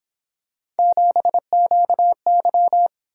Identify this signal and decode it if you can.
Morse code, heard as a keyed tone, about 26 wpm, 700 Hz. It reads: 7QY
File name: Speed-Racing - Three Letter-Number Combinations - 22wpm